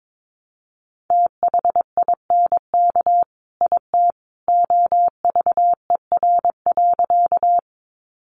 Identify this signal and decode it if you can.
Morse code, heard as a keyed tone, about 22 wpm, 700 Hz. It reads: THINK IT OVER.
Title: Sentences from Top 200 Words 22wpm